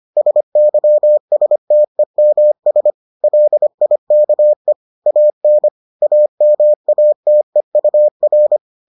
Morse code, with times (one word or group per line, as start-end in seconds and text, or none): 0.2-2.9 SYSTEMS
3.2-4.7 LIKE
5.1-5.7 AN
6.0-8.6 AMATEUR